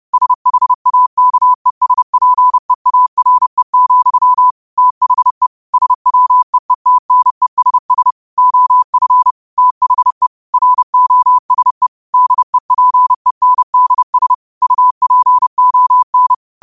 Morse code, heard as a keyed tone, about 30 wpm, 1000 Hz. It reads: SHAKESPEARE, THE SWEETNESS OF THE ROSE DEPENDS UPON
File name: qotdfiles / 30wpm